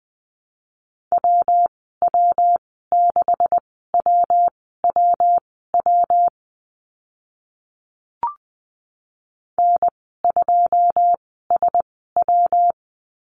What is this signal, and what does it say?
Morse code, heard as a keyed tone, about 20 wpm, spaced as for 14 wpm, 700 Hz. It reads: WW6WWW N2SW
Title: 140 Lesson 23g - Letter f - Call Signs 20wpm